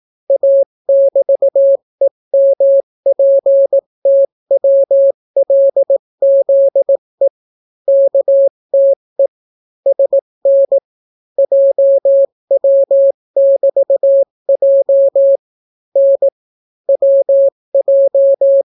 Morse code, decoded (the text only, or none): A<BT>EMPTWLZE KTE SN JW<BT>J N WJ